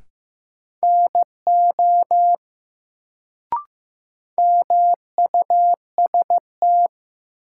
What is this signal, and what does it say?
Morse code, heard as a keyed tone, about 15 wpm, 700 Hz. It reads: NO MUST
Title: Sets of 2 Words - New Word Drill - 15wpm